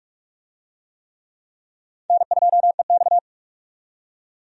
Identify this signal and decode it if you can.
Morse code, heard as a keyed tone, about 41 wpm, 700 Hz. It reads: N2EX
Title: USA Call Signs - 4 Characters - 45wpm